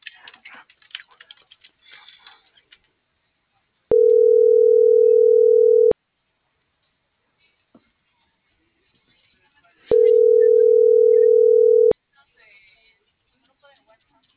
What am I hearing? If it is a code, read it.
Ringback tone (North America), 440 + 480 Hz.